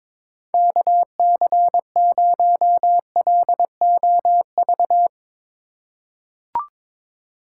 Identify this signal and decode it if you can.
Morse code, heard as a keyed tone, about 22 wpm, 700 Hz. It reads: KC0LOV
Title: USA Call Signs - 6 Characters - 4x Character Spacing - 22wpm